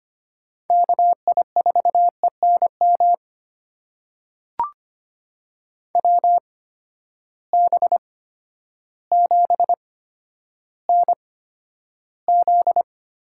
Morse code, heard as a keyed tone, about 25 wpm, spaced as for 11 wpm, 700 Hz. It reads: KI4ENM WB7NZ
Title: USA Call Signs - 6 Characters - 8x Character Spacing - 25wpm